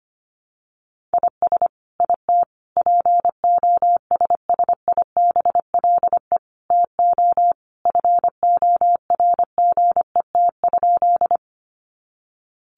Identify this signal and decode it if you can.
Morse code, heard as a keyed tone, about 25 wpm, 700 Hz. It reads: IS IT POSSIBLE TO FORGET?
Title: Sentences from Top 1200 Words 25wpm